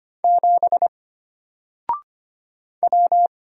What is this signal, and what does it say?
Morse code, heard as a keyed tone, about 25 wpm, 700 Hz. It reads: 7 W
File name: Single-Letter-Number-25wpm